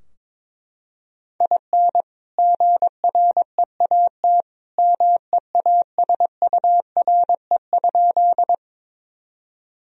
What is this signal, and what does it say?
Morse code, heard as a keyed tone, about 22 wpm, 700 Hz. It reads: IN GREAT MEASURE?
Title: Sentences from Top 1000 Words - Review Part 2 - 22wpm